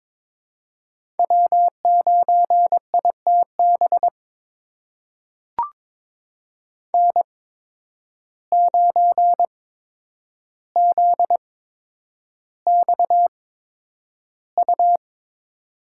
Morse code, heard as a keyed tone, about 22 wpm, spaced as for 9 wpm, 700 Hz. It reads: W9ITB N9ZXU
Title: USA Call Signs - 8x Character Spacing - 22wpm